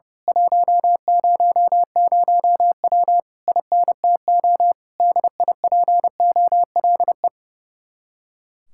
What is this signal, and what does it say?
Morse code, heard as a keyed tone, about 28 wpm, 700 Hz. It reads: 100W INTO DIPOLE
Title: Speed-Racing Common QSO Phrases - 30wpm